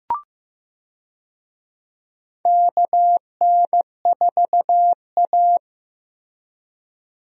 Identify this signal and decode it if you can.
Morse code, heard as a keyed tone, about 15 wpm, 700 Hz. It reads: KN4A
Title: USA Call Signs - 4 Characters - Repeated 3x - 15wpm